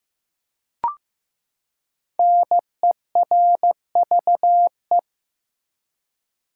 Morse code, heard as a keyed tone, about 15 wpm, 700 Hz. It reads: NERVE
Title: Five Letter Words - 15wpm